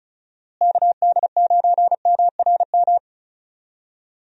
Morse code, heard as a keyed tone, about 34 wpm, 700 Hz. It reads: KD9MRM
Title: USA Call Signs - 6 Characters - Repeated 2x - 35wpm